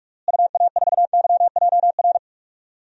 Morse code, heard as a keyed tone, about 46 wpm, 700 Hz. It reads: UA4YJR
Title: DXCC-100 Call Signs - 6 Characters - Repeated 3x - 45wpm